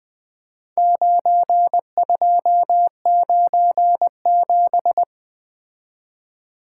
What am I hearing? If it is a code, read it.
Morse code, heard as a keyed tone, about 21 wpm, 700 Hz. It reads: 9297
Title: Four Letter-Number Combinations - 20wpm